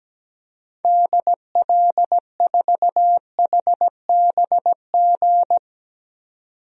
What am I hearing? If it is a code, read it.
Morse code, heard as a keyed tone, about 17 wpm, 700 Hz. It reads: DL4HBG
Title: DXCC-100 Call Signs - 6 Characters - 17wpm